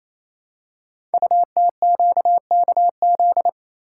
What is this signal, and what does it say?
Morse code, heard as a keyed tone, about 27 wpm, 700 Hz. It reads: UTQKZ
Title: Code Groups - 5 Letters - 28wpm